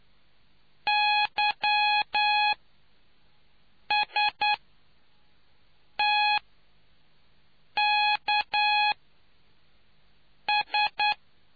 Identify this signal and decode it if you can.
Morse code, heard as a keyed tone, about 10 wpm, spaced as for 5 wpm, 840 Hz. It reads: YSTKS